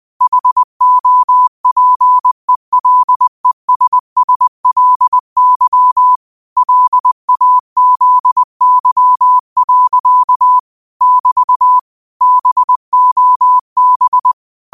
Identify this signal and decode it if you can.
Morse code, heard as a keyed tone, about 20 wpm, 1000 Hz. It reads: HOPELESSLY LAZY. <BT> BOB
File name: 20wpm